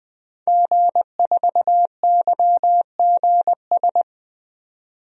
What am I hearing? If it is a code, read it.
Morse code, heard as a keyed tone, about 20 wpm, 700 Hz. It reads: G4YGS
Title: DXCC-100 Call Signs - 5 Characters - 20wpm